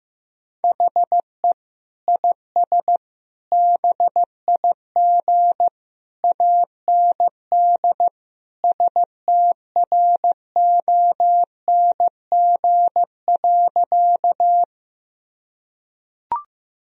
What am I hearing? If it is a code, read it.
Morse code, heard as a keyed tone, about 15 wpm, 700 Hz. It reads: HE IS BIG AND STRONG.
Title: Sentences from Top 1000 Words - Review Part 3 - 15wpm